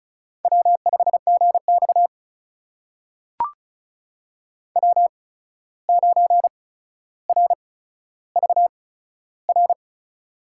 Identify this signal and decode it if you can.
Morse code, heard as a keyed tone, about 36 wpm, spaced as for 14 wpm, 700 Hz. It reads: W5GX W9RVR
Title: USA Call Signs - 8x Character Spacing - 35wpm